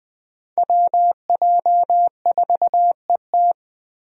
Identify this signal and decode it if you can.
Morse code, heard as a keyed tone, about 20 wpm, 700 Hz. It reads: WJ4ET
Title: USA Call Signs - 5 Characters - 20wpm